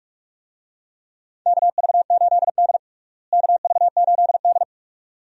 Morse code, heard as a keyed tone, about 46 wpm, 700 Hz. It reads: KV8D KV8D
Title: USA Call Signs - 4 Characters - Repeated 2x - 45wpm